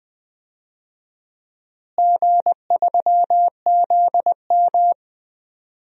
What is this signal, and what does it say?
Morse code, heard as a keyed tone, about 20 wpm, 700 Hz. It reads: G3ZM
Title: DXCC-100 Call Signs - 4 Characters - Repeated 3x - 20wpm